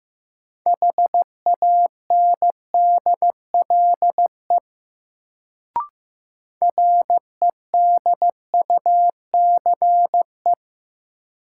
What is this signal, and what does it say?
Morse code, heard as a keyed tone, about 15 wpm, 700 Hz. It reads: HANDLE REDUCE
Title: Sentences from Top 1100 Words - New Word Drill - 15wpm